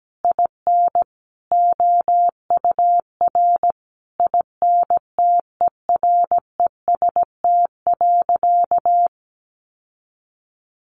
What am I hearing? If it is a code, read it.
Morse code, heard as a keyed tone, about 17 wpm, 700 Hz. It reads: IN OUR INTEREST.
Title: Sentences from Top 1300 Words 17wpm